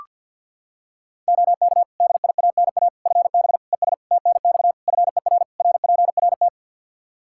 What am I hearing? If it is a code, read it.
Morse code, heard as a keyed tone, about 45 wpm, 700 Hz. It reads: OK DIANA FB ES TNX FER RPRT
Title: Practice QSO - Exchange 2 - Info - 50wpm